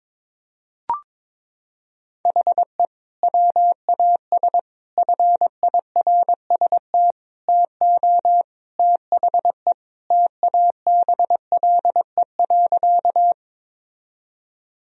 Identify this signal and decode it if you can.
Morse code, heard as a keyed tone, about 22 wpm, 700 Hz. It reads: HE WAS FIRST TO THE TABLE.
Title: Sentences from Top 1000 Words - Review Part 2 - 22wpm